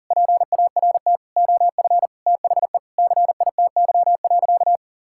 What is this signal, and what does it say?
Morse code, heard as a keyed tone, about 40 wpm, 700 Hz. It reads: PART OF THE CITY.
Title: Sentences from Top 1000 Words - Review Part 5 - 40wpm